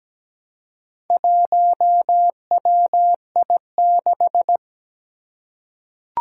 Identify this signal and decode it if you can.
Morse code, heard as a keyed tone, about 17 wpm, 700 Hz. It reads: 1WI6
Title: Four Challenging Letter-Number Combinations 17wpm